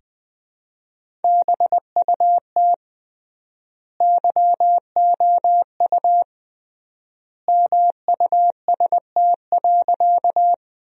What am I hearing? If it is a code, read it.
Morse code, heard as a keyed tone, about 20 wpm, 700 Hz. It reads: BUT YOU MUST.
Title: 3x Word-Spacing - Sentences from Top 200 Words - 20wpm